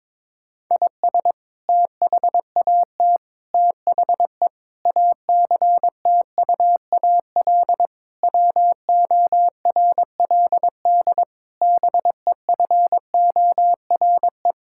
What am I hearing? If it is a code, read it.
Morse code, heard as a keyed tone, about 22 wpm, 700 Hz. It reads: IS THAT THE ACTUAL WORLD BEFORE
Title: Sentences from Top 1000 Words - Review Part 5 - 22wpm